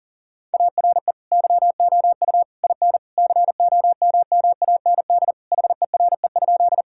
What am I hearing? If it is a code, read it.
Morse code, heard as a keyed tone, about 40 wpm, 700 Hz. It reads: ARE YOU IN COMMAND HERE?
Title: Sentences from Top 1600 Words 40wpm